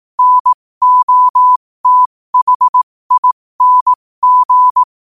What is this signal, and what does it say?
Morse code, heard as a keyed tone, about 18 wpm, spaced as for 15 wpm, 1000 Hz. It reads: NOTHING
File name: qotdfiles / 15wpm